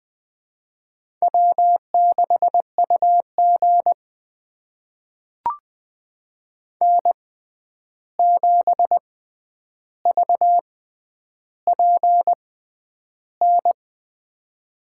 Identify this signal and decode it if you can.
Morse code, heard as a keyed tone, about 20 wpm, spaced as for 10 wpm, 700 Hz. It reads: W6UG N7VPN
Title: USA Call Signs - 6x Character Spacing - 20wpm